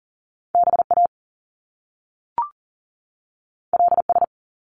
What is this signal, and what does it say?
Morse code, heard as a keyed tone, about 41 wpm, 700 Hz. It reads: BA LS
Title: Top Two Letter Combinations in Words 40wpm